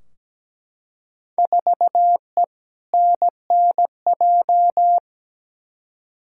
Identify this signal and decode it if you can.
Morse code, heard as a keyed tone, about 17 wpm, 700 Hz. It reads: 4E NNJ
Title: Field Day Exchanges - 17wpm